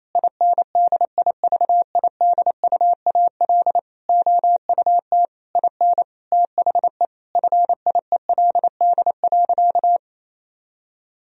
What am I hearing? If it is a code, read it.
Morse code, heard as a keyed tone, about 28 wpm, 700 Hz. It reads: INDIVIDUAL OUT IN THE FIELD.
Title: Sentences from Top 1000 Words - Review Part 4 - 28wpm